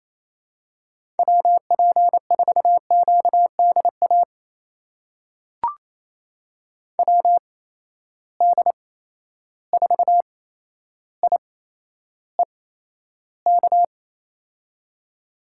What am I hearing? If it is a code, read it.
Morse code, heard as a keyed tone, about 28 wpm, spaced as for 12 wpm, 700 Hz. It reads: WP4QDA WD4IEK
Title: USA Call Signs - 6 Characters - 8x Character Spacing - 28wpm